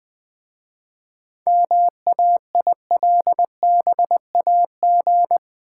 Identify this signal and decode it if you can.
Morse code, heard as a keyed tone, about 20 wpm, 700 Hz. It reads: MAILBAG
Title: Easy Words with Hints - 20wpm